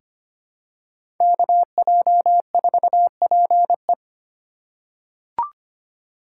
Morse code, heard as a keyed tone, about 25 wpm, 700 Hz. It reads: KJ4PE
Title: USA Call Signs - 5 Characters - 3x Character Spacing - 25wpm